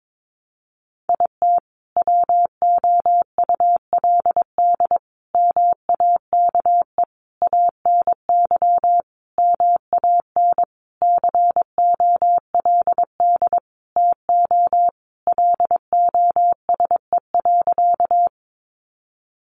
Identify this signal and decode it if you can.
Morse code, heard as a keyed tone, about 22 wpm, 700 Hz. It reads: IT WOULD MAKE ANY MAN COLD TO LOSE.
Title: Sentences from Top 1100 Words 22wpm